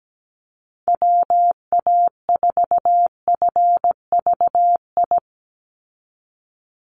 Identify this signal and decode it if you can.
Morse code, heard as a keyed tone, about 17 wpm, 700 Hz. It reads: WA4FVI
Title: USA Call Signs 17wpm